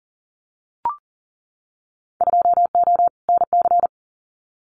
Morse code, heard as a keyed tone, about 42 wpm, 700 Hz. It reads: 2O NC
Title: Winter Field Day Exchanges 40wpm